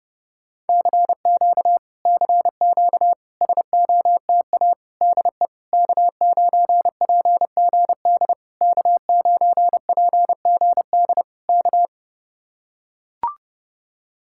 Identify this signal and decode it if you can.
Morse code, heard as a keyed tone, about 32 wpm, 700 Hz. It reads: CQ CQ SOTA DE K9PGD K9PGD K